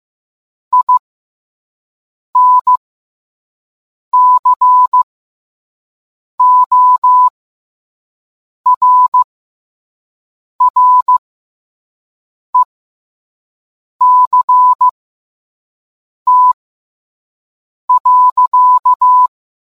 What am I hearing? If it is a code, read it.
Morse code, heard as a keyed tone, about 15 wpm, spaced as for 5 wpm, 1000 Hz. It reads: INCORRECT.